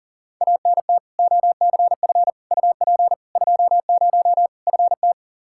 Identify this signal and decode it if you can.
Morse code, heard as a keyed tone, about 40 wpm, 700 Hz. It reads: ANT OCF UP 20 FT
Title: Practice QSO - Exchange 2 - Info - 40wpm